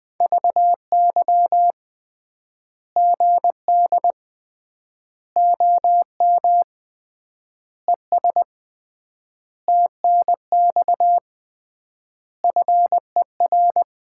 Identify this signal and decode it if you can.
Morse code, heard as a keyed tone, about 20 wpm, 700 Hz. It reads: VY GD OM ES TNX FER
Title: Common QSO Phrases - 3x Word-Spacing - 20wpm